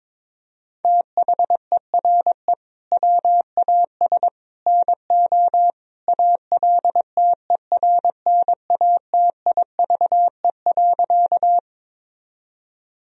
Morse code, heard as a keyed tone, about 22 wpm, 700 Hz. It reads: THERE WAS NO ALTERNATIVE.
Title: Sentences from Top 1000 Words - Review Part 4 - 22wpm